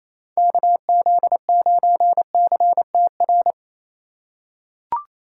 Morse code, heard as a keyed tone, about 28 wpm, 700 Hz. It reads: KZ9CTR
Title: USA Call Signs - 6 Characters - 4x Character Spacing - 28wpm